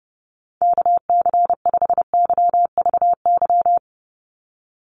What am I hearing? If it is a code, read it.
Morse code, heard as a keyed tone, about 30 wpm, 700 Hz. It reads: KC5YVY
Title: USA Call Signs 30wpm